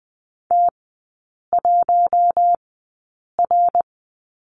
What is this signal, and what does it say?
Morse code, heard as a keyed tone, about 20 wpm, spaced as for 8 wpm, 700 Hz. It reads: T1R
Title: 100 Lesson 17e - Letter u - Three Characters 20wpm